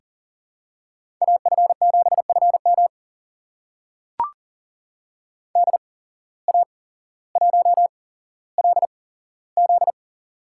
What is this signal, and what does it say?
Morse code, heard as a keyed tone, about 42 wpm, spaced as for 17 wpm, 700 Hz. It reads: AF7FM DA1LZ
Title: USA Call Signs - 5 Characters - 8x Character Spacing - 40wpm